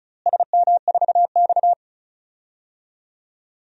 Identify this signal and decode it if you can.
Morse code, heard as a keyed tone, about 34 wpm, 700 Hz. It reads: SM4X
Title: DXCC-100 Call Signs - 4 Characters - Repeated 3x - 35wpm